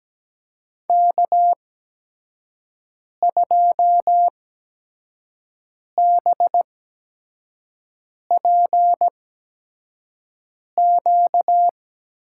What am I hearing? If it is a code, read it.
Morse code, heard as a keyed tone, about 17 wpm, spaced as for 5 wpm, 700 Hz. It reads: K2BPQ
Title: USA Call Signs - 5 Characters - 8x Character Spacing - 17wpm